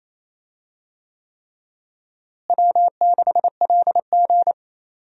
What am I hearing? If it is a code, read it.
Morse code, heard as a keyed tone, about 28 wpm, 700 Hz. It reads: W6LG
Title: USA Call Signs - 4 Characters - Repeated 3x - 28wpm